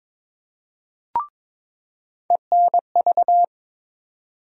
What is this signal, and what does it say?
Morse code, heard as a keyed tone, about 21 wpm, 700 Hz. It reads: ENV
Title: USA Call Sign Postfixes - 3 Letters 22wpm